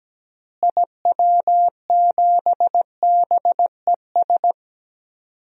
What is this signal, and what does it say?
Morse code, heard as a keyed tone, about 17 wpm, 700 Hz. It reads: IW7BES
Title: DXCC-100 Call Signs - 6 Characters - Repeated 3x - 17wpm